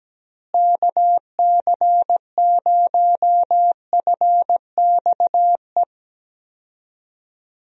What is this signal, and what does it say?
Morse code, heard as a keyed tone, about 17 wpm, 700 Hz. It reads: KC0FXE
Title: USA Call Signs - 6 Characters - Repeated 2x - 17wpm